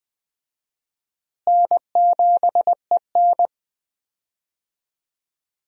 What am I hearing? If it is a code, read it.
Morse code, heard as a keyed tone, about 20 wpm, 700 Hz. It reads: N7EN
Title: USA Call Signs - 4 Characters - Repeated 3x - 20wpm